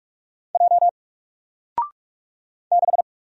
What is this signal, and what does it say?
Morse code, heard as a keyed tone, about 44 wpm, 700 Hz. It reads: J 6
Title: Single Character (Letter:Number:Punctuation:Prosign) 45wpm